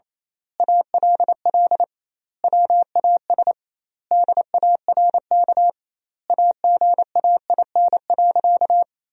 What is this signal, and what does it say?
Morse code, heard as a keyed tone, about 28 wpm, 700 Hz. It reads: ALL WAS DARK AGAIN.
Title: 2x Word-Spacing - Sentences from Top 500 Words - 28wpm